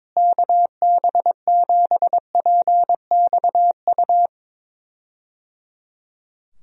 Morse code, heard as a keyed tone, about 22 wpm, 700 Hz. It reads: KB7PXU
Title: USA Call Signs - 6 Characters - Repeated 3x - 22wpm